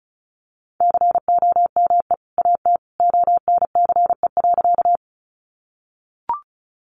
Morse code, heard as a keyed tone, about 34 wpm, 700 Hz. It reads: COME AT ONCE.
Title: Sentences from Top 300 Words 35wpm